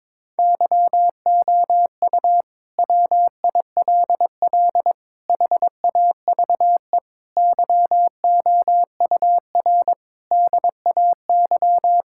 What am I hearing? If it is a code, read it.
Morse code, heard as a keyed tone, about 22 wpm, 700 Hz. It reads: YOU WILL HAVE YOUR DAY
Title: Sentences from Top 1600 Words 22wpm